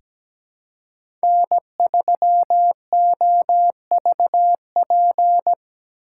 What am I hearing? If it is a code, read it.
Morse code, heard as a keyed tone, about 17 wpm, 700 Hz. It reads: N3OVP
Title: USA Call Signs - 5 Characters - 3x Character Spacing - 17wpm